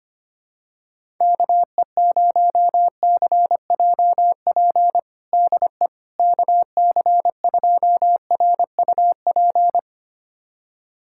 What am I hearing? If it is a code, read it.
Morse code, heard as a keyed tone, about 25 wpm, 700 Hz. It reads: KE0CJP DE KC2RUP
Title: USA Callsign DE Callsign - 6 Characters - 25wpm